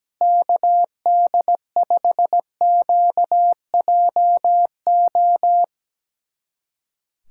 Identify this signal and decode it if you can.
Morse code, heard as a keyed tone, about 17 wpm, 700 Hz. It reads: KD5QJO
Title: USA Callsign DE Callsign 17wpm